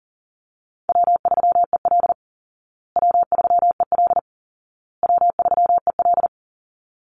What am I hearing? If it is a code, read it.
Morse code, heard as a keyed tone, about 40 wpm, 700 Hz. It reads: W3EL W3EL W3EL
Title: Call Signs Repeated 2x - 40wpm